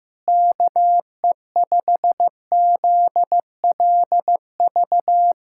Morse code, heard as a keyed tone, about 15 wpm, 700 Hz. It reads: KE5ZLV
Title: USA Callsign DE Callsign 15wpm